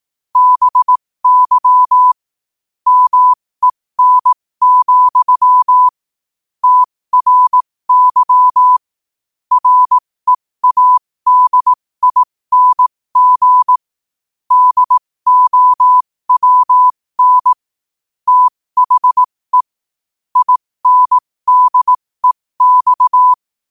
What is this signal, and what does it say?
Morse code, heard as a keyed tone, about 18 wpm, spaced as for 15 wpm, 1000 Hz. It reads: BY MEN, TRY READING DOWN THE INDEX
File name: qotdfiles / 15wpm